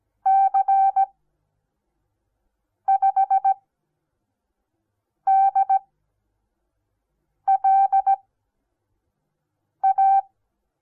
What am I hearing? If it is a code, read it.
Morse code, heard as a keyed tone, about 17 wpm, spaced as for 5 wpm, 780 Hz. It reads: C5DLA
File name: Onlinelehrgang / morsen / DJ4UF Morsen08